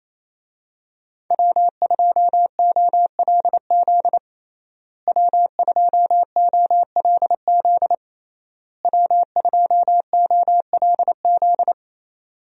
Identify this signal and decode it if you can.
Morse code, heard as a keyed tone, about 28 wpm, 700 Hz. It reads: W2OLZ W2OLZ W2OLZ
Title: USA Call Signs - 5 Characters - Repeated 3x - 28wpm